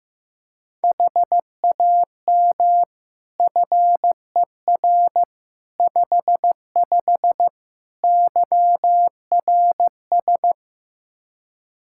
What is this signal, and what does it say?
Morse code, heard as a keyed tone, about 16 wpm, 700 Hz. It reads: HAM FER 55 YRS
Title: Practice QSO - Exchange 3 - More Info - 15wpm